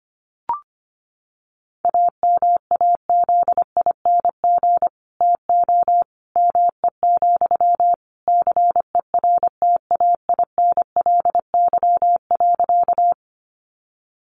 Morse code, heard as a keyed tone, about 25 wpm, 700 Hz. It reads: AMAZING TO ME, CERTAINLY.
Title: Sentences from Top 1500 Words 25wpm